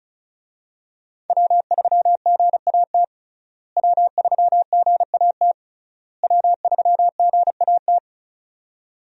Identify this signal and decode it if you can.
Morse code, heard as a keyed tone, about 36 wpm, 700 Hz. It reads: W3GAT W3GAT W3GAT
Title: USA Call Signs - 5 Characters - Repeated 3x - 35wpm